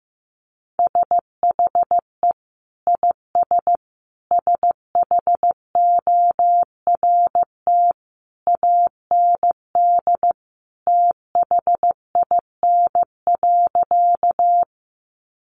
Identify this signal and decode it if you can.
Morse code, heard as a keyed tone, about 15 wpm, 700 Hz. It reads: SHE IS SHORT AND THIN.
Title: Sentences from Top 1200 Words 15wpm